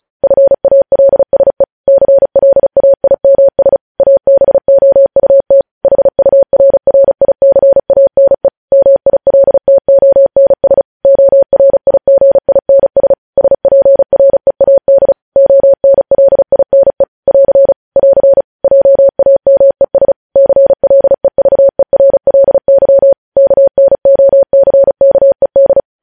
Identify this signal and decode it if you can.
Morse code, heard as a keyed tone, about 36 wpm, 550 Hz. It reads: FALSE CLAIMS ABOUT HURRICANE MILTONS ORIGINS SPREAD ONLINE <AR> <AR> JAMES CLEVERLY KNOCKED